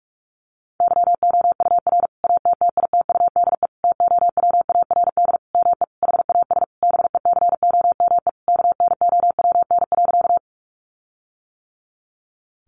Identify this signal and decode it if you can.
Morse code, heard as a keyed tone, about 44 wpm, 700 Hz. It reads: YOUR ATTITUDE TOWARD ME HAS BECOME KNOWN.
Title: Sentences from Top 1300 Words 45wpm